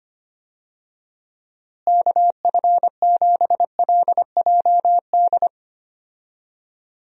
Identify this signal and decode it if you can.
Morse code, heard as a keyed tone, about 25 wpm, 700 Hz. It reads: KF7LJD
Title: USA Call Signs - 6 Characters - Repeated 2x - 25wpm